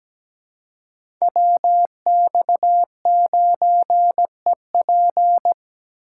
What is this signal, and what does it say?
Morse code, heard as a keyed tone, about 17 wpm, 700 Hz. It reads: WX9EP
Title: USA Call Signs - 5 Characters - 4x Character Spacing - 17wpm